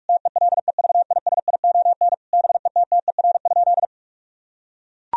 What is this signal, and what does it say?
Morse code, heard as a keyed tone, about 41 wpm, 700 Hz. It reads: TELEVISION BETTER?
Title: Sentences from Top 1000 Words - Review Part 4 - 45wpm